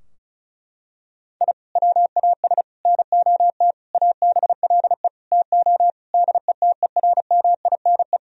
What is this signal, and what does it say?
Morse code, heard as a keyed tone, about 34 wpm, 700 Hz. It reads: I WAS NOT ABLE TO DETERMINE
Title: Sentences from Top 1000 Words - Review Part 4 - 35wpm